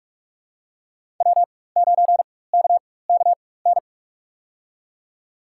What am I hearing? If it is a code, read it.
Morse code, heard as a keyed tone, about 42 wpm, spaced as for 21 wpm, 700 Hz. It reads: W9KKN
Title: USA Call Signs - 4x Character Spacing - 45wpm